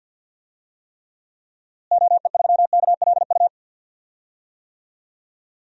Morse code, heard as a keyed tone, about 51 wpm, 700 Hz. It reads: OE3KLU
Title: DXCC-100 Call Signs - 6 Characters - Repeated 3x - 50wpm